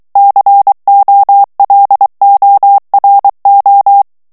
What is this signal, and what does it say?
Morse code, heard as a keyed tone, about 23 wpm, 800 Hz. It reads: COLORO
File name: ITALIEN texte clair 2 vitesse 22 mots minute